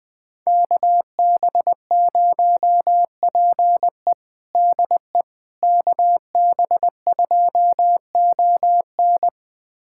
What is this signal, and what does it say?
Morse code, heard as a keyed tone, about 21 wpm, 700 Hz. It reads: KB0PE DE KB2ON
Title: USA Callsign DE Callsign - 5 Characters - 20wpm